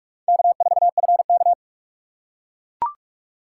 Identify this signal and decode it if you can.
Morse code, heard as a keyed tone, about 44 wpm, 700 Hz. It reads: K4FK